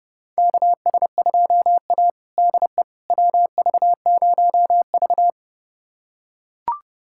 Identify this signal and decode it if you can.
Morse code, heard as a keyed tone, about 29 wpm, 700 Hz. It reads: KS2A DE WV0V
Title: USA Callsign DE Callsign - 4 Characters - 30wpm